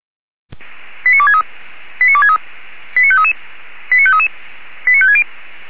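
Five-tone selective call (ZVEI), heard as quick runs of five tones, tones of about 80 ms. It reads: 89263 89263 89533 89533 89560
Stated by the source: ZVEI five-tone selective call received on an SDR.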